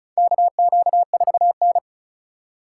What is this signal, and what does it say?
Morse code, heard as a keyed tone, about 36 wpm, 700 Hz. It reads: KQ4N
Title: USA Call Signs - 4 Characters - 2x Character Spacing - 35wpm